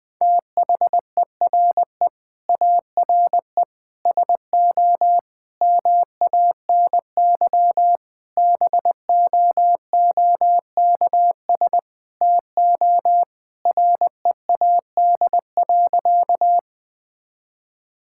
Morse code, heard as a keyed tone, about 20 wpm, 700 Hz. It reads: THERE ARE SO MANY BOOKS TO READ.
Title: Sentences with 4-Word Phrases - 20wpm